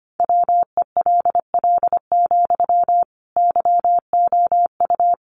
Morse code, heard as a keyed tone, about 25 wpm, 700 Hz. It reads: WELL, YOU
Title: Sentences from Top 700 Words 25wpm